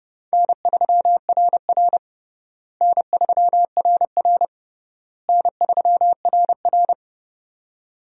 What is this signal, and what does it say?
Morse code, heard as a keyed tone, about 30 wpm, 700 Hz. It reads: N3RR N3RR N3RR
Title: USA Call Signs - 4 Characters - Repeated 3x - 30wpm